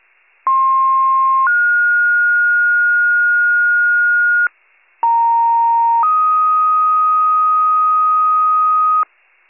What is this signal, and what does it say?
Two-tone page, heard as a one-second tone, then a three-second tone, about 1060 then 1430 Hz, about 930 then 1250 Hz, with noise. It is heard from 0.5 to 9.0 s.